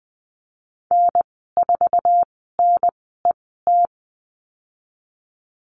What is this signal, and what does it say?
Morse code, heard as a keyed tone, about 19 wpm, spaced as for 14 wpm, 700 Hz. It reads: N4NET
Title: 055 Lesson 10g - Letter r - Call Signs 20wpm